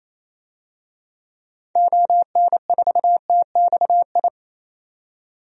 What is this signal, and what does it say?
Morse code, heard as a keyed tone, about 28 wpm, 700 Hz. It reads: ON4TXI
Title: DXCC-100 Call Signs - 6 Characters - 28wpm